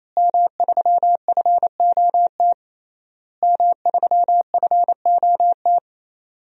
Morse code, heard as a keyed tone, about 28 wpm, 700 Hz. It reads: M3FOT M3FOT
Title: DXCC-100 Call Signs - 5 Characters - Repeated 3x - 28wpm